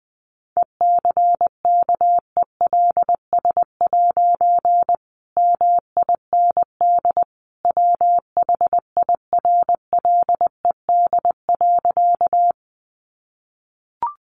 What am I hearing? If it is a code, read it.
Morse code, heard as a keyed tone, about 20 wpm, 700 Hz. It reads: ECKELS' MIND WHIRLED.
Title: A Sound Of Thunder 20wpm